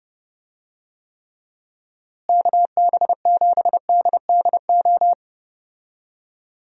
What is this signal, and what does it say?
Morse code, heard as a keyed tone, about 30 wpm, 700 Hz. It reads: KB7DDO
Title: USA Call Signs - 6 Characters - Repeated 2x - 30wpm